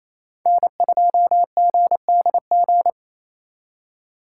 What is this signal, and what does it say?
Morse code, heard as a keyed tone, about 36 wpm, 700 Hz. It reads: N2GDG